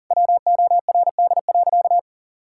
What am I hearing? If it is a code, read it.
Morse code, heard as a keyed tone, about 41 wpm, 700 Hz. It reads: WORD.
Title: Sentences from Top 1000 Words - Review Part 6 - 40wpm